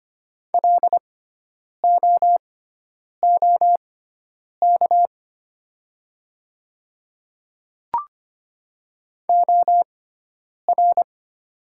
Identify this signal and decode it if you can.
Morse code, heard as a keyed tone, about 25 wpm, spaced as for 9 wpm, 700 Hz. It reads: LOOK OR
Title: 6x Character Spacing - Top 100 Words - 25wpm